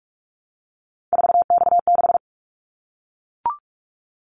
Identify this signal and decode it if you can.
Morse code, heard as a keyed tone, about 44 wpm, 700 Hz. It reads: 4X6
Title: Callsign Prefixes for Top 50 DXCC - 45wpm